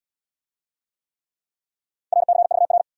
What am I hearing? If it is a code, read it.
Morse code, heard as a keyed tone, about 68 wpm, 700 Hz. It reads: RCVR